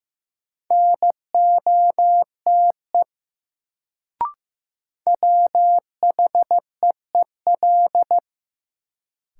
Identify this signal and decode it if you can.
Morse code, heard as a keyed tone, about 16 wpm, 700 Hz. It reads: NOTE WHEEL
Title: Sentences from Top 500 Words - New Word Drill - 15wpm